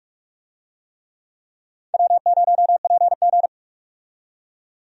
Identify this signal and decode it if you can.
Morse code, heard as a keyed tone, about 46 wpm, 700 Hz. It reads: W0PG
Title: USA Call Signs - 4 Characters - 45wpm